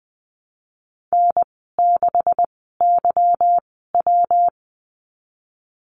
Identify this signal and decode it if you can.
Morse code, heard as a keyed tone, about 20 wpm, spaced as for 14 wpm, 700 Hz. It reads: N6YW
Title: 167 Lesson 27g - Number 7 - Call Signs 20wpm